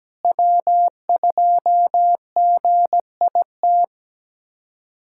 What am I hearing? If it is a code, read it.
Morse code, heard as a keyed tone, about 17 wpm, 700 Hz. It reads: W2GIT
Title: USA Callsign DE Callsign - 5 Characters - 17wpm